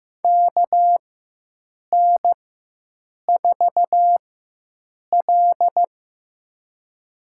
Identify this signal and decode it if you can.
Morse code, heard as a keyed tone, about 15 wpm, spaced as for 7 wpm, 700 Hz. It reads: KN4L
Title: USA Call Signs - 4x Character Spacing - 15wpm